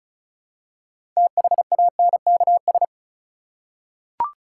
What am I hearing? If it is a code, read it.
Morse code, heard as a keyed tone, about 36 wpm, 700 Hz. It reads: THANKS
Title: Six Letter Words - 35wpm